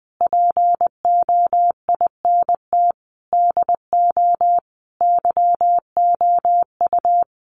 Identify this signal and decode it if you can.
Morse code, heard as a keyed tone, about 20 wpm, 700 Hz. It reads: POINT DO YOU
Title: Sentences from Top 1300 Words 20wpm